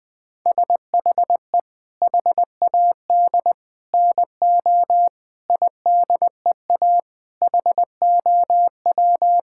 Morse code, heard as a keyed tone, about 20 wpm, 700 Hz. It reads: SHE HAD NO IDEA HOW
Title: Sentences from Top 1000 Words - Review Part 6 - 20wpm